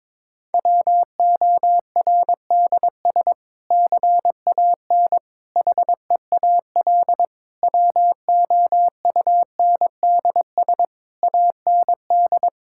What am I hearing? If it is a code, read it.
Morse code, heard as a keyed tone, about 22 wpm, 700 Hz. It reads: WORDS CAN HEAL WOUNDS AND
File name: Difficult Sentences with Hints - 22wpm